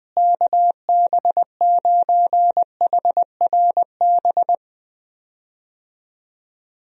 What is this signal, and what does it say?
Morse code, heard as a keyed tone, about 21 wpm, 700 Hz. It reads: KB9HRB
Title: USA Call Signs - 6 Characters - Repeated 2x - 20wpm